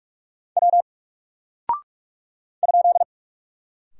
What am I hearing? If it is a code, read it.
Morse code, heard as a keyed tone, about 44 wpm, 700 Hz. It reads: W ?
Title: Single Character (Letter:Number:Punctuation:Prosign) 45wpm